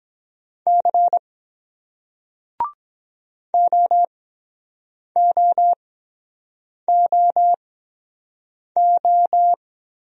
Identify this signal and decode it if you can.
Morse code, heard as a keyed tone, about 23 wpm, spaced as for 7 wpm, 700 Hz. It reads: C OOOO